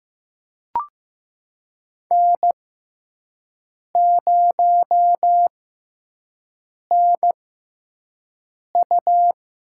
Morse code, heard as a keyed tone, about 15 wpm, spaced as for 5 wpm, 700 Hz. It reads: N0NU
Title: USA Call Signs - 5 Characters - 6x Character Spacing - 15wpm